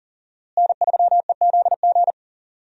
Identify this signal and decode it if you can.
Morse code, heard as a keyed tone, about 40 wpm, 700 Hz. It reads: N3EZG
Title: USA Call Signs - 5 Characters - Repeated 3x - 40wpm